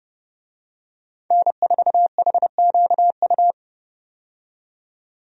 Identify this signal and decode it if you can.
Morse code, heard as a keyed tone, about 30 wpm, 700 Hz. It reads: N4HQU
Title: USA Call Signs - 5 Characters - 30wpm